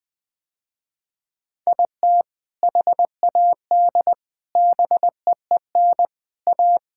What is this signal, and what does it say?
Morse code, heard as a keyed tone, about 21 wpm, 700 Hz. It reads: IT HAD BEEN A
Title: Sentences with 2-Word Phrases - Spoken Beginning - 20wpm